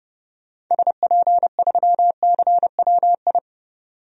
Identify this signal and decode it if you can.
Morse code, heard as a keyed tone, about 30 wpm, 700 Hz. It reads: SP3CWI
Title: DXCC-100 Call Signs - 6 Characters - 30wpm